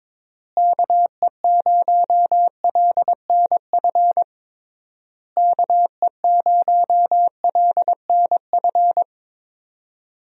Morse code, heard as a keyed tone, about 22 wpm, 700 Hz. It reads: KE0LNF KE0LNF
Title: USA Call Signs - 6 Characters - Repeated 3x - 22wpm